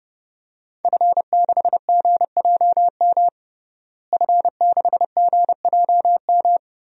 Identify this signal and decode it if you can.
Morse code, heard as a keyed tone, about 32 wpm, 700 Hz. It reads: F6GJM F6GJM
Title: DXCC-100 Call Signs - 5 Characters - Repeated 3x - 30wpm